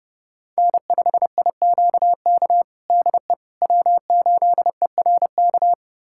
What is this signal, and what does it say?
Morse code, heard as a keyed tone, about 30 wpm, 700 Hz. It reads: N5IQK DE W8ERK
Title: USA Callsign DE Callsign - 5 Characters - 30wpm